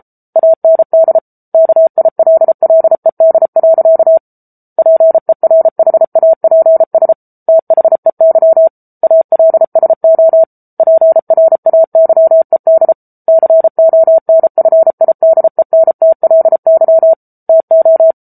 Morse code, heard as a keyed tone, about 34 wpm, 650 Hz. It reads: AND KILLED. PERHAPS THEY ALSO PRAYED CONFIDENTLY TO
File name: Book Two / 07-Chapter 7 - 33 WPM